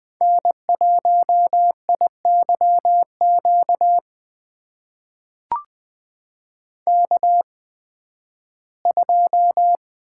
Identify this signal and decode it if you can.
Morse code, heard as a keyed tone, about 20 wpm, spaced as for 13 wpm, 700 Hz. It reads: N1IYQ K2
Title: USA Call Signs - 5 Characters - 8x Character Spacing - 20wpm